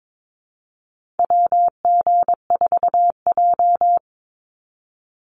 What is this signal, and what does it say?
Morse code, heard as a keyed tone, about 22 wpm, 700 Hz. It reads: WG4J
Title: USA Call Signs 22wpm